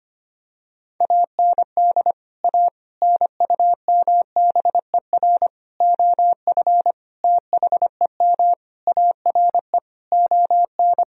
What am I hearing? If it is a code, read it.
Morse code, heard as a keyed tone, about 25 wpm, 700 Hz. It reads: AND A NUMBER OF THEM ARE ON
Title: Sentences from Top 1000 Words - Review Part 1 - 25wpm